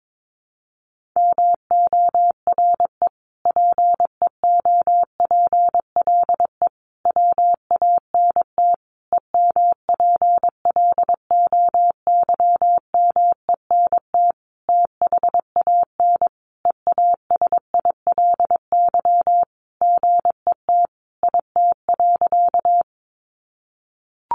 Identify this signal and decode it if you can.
Morse code, heard as a keyed tone, about 22 wpm, 700 Hz. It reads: MORE PEOPLE WANT EMPLOYMENT THAN EASILY GET IT.
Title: Sentences from Top 1500 Words 22wpm